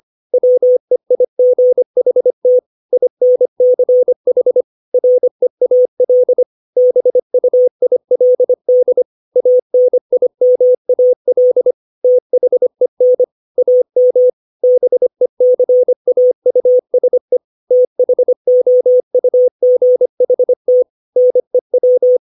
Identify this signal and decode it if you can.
Morse code, heard as a keyed tone, about 25 wpm, 500 Hz. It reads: WEIGHT INCH REAL BUILD ANIMAL THEN AM BECAUSE THOUGHT NEW
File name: 632 words / 632 Words 25WPM